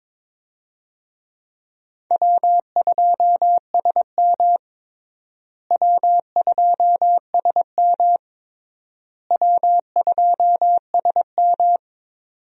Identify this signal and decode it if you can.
Morse code, heard as a keyed tone, about 22 wpm, 700 Hz. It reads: W2SM W2SM W2SM
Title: USA Call Signs - 4 Characters - Repeated 3x - 22wpm